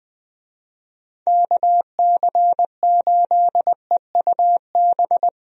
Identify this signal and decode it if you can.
Morse code, heard as a keyed tone, about 21 wpm, 700 Hz. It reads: KC8EUB
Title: USA Call Signs - 6 Characters - Repeated 2x - 20wpm